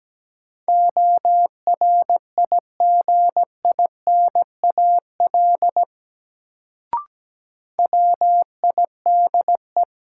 Morse code, heard as a keyed tone, about 17 wpm, 700 Hz. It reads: ORIGINAL WIDE
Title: Sentences from Top 900 Words - New Word Drill - 17wpm